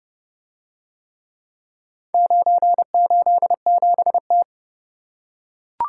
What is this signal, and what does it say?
Morse code, heard as a keyed tone, about 29 wpm, 700 Hz. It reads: 987T